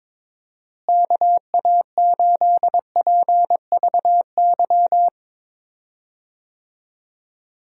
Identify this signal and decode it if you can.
Morse code, heard as a keyed tone, about 22 wpm, 700 Hz. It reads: KA8PVY
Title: USA Call Signs - 6 Characters - Repeated 3x - 22wpm